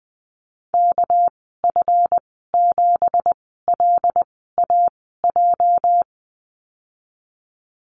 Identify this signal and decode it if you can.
Morse code, heard as a keyed tone, about 19 wpm, spaced as for 14 wpm, 700 Hz. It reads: KF7LAJ
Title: 213 Lesson 34g - Number 8 - Call Signs 20wpm